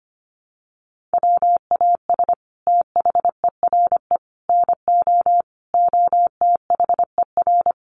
Morse code, heard as a keyed tone, about 25 wpm, 700 Hz. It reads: WAS THERE NO OTHER
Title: Sentences from Top 1400 Words 25wpm